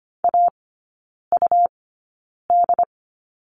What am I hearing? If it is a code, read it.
Morse code, heard as a keyed tone, about 25 wpm, spaced as for 9 wpm, 700 Hz. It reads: AUD
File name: Lesson 17e - Letter u - Three Characters 25wpm